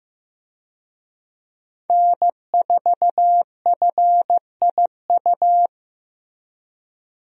Morse code, heard as a keyed tone, about 15 wpm, 700 Hz. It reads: N4FIU
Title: USA Call Signs - 5 Characters - Repeated 2x - 15wpm